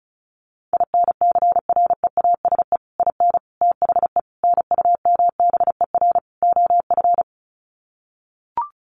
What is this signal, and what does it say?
Morse code, heard as a keyed tone, about 34 wpm, 700 Hz. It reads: INCREASE IN THE NUMBER OF
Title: Sets of 5 Words 35wpm